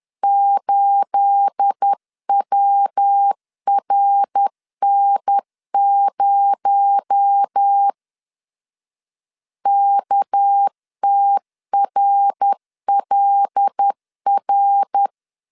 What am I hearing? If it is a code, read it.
Morse code, heard as a keyed tone, about 10 wpm, 800 Hz. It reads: 8WRN0 KTRLR
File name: CW / DET / 1 9 1500C S B 10wpm 4439